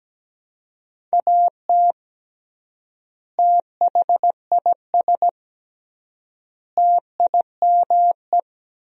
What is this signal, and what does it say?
Morse code, heard as a keyed tone, about 17 wpm, 700 Hz. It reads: AT THIS TIME
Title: 3x Word-Spacing - Sentences from Top 300 Words - 17wpm